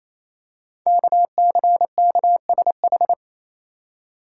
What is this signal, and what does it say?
Morse code, heard as a keyed tone, about 28 wpm, 700 Hz. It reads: KCKSH
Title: Code Groups - 5 Letters - 28wpm